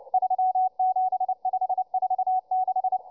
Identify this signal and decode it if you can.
Morse code, heard as a keyed tone, about 29 wpm, 750 Hz, with radio noise. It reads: 37546